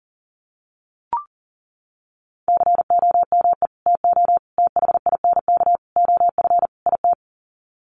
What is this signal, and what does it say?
Morse code, heard as a keyed tone, about 40 wpm, 700 Hz. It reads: COME TO THINK OF IT
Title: Sets of 5 Words Encore 40wpm